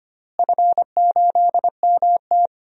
Morse code, heard as a keyed tone, about 25 wpm, 700 Hz. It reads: F8MT
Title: DXCC-100 Call Signs - 4 Characters - Repeated 2x - 25wpm